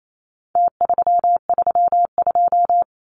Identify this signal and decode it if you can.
Morse code, heard as a keyed tone, about 28 wpm, 700 Hz. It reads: T332
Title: Numbers - Contest Style - 4-Digit 28wpm